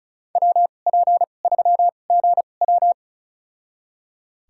Morse code, heard as a keyed tone, about 34 wpm, spaced as for 25 wpm, 700 Hz. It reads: WP3GW
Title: USA Call Signs - 5 Characters - 2x Character Spacing - 35wpm